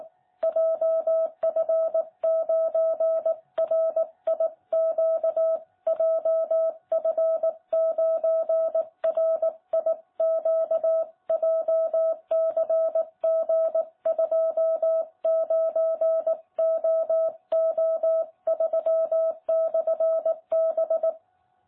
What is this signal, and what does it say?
Morse code, heard as a keyed tone, about 18 wpm, 660 Hz. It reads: JF9RIQ JF9RIQ JCG29OO3/B
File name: jh8cbh / 2021cw